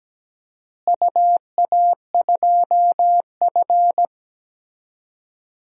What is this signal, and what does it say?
Morse code, heard as a keyed tone, about 17 wpm, 700 Hz. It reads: UA2F
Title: DXCC-100 Call Signs - 4 Characters - Repeated 3x - 17wpm